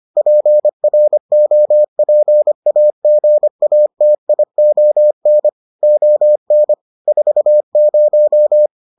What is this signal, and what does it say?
Morse code, heard as a keyed tone, about 25 wpm, 600 Hz. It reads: PROPAGATION ON 40